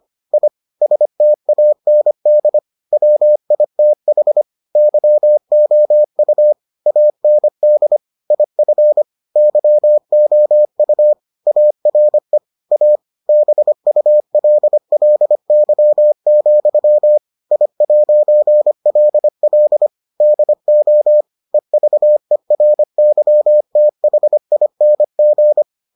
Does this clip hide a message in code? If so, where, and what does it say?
Yes: Morse code from 0.3 to 25.6 s, heard as a keyed tone, about 25 wpm, 600 Hz. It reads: I STAND WITH YOU AND IF YOU ARE A BULLY, I'LL DO EVERYTHING